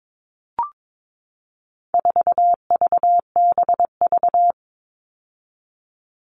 Morse code, heard as a keyed tone, about 22 wpm, 700 Hz. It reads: 4VBV
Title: All Cut Numbers - 4-Digit 22wpm